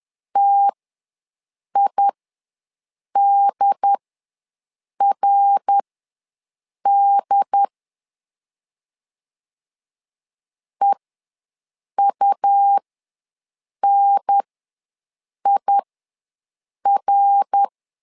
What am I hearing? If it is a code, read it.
Morse code, heard as a keyed tone, about 10 wpm, spaced as for 6 wpm, 800 Hz. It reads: TIDRD EUNIR
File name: LEZIONI / 3 MWRO / 3 MWR ws 10wpm 2151